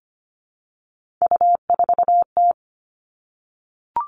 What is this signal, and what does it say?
Morse code, heard as a keyed tone, about 25 wpm, 700 Hz. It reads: U4T
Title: All Cut Numbers - 3-Digit 25wpm